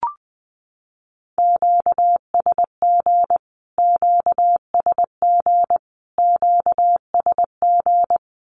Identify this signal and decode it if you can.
Morse code, heard as a keyed tone, about 20 wpm, 700 Hz. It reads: QSG QSG QSG
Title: Q-Codes Repeated 3x 20wpm